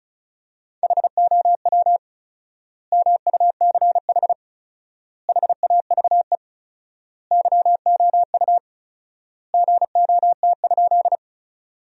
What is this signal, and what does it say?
Morse code, heard as a keyed tone, about 36 wpm, 700 Hz. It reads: HOW MUCH HAVE YOU GOT?
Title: 4x Word-Spacing - Sentences from Top 300 Words - 35wpm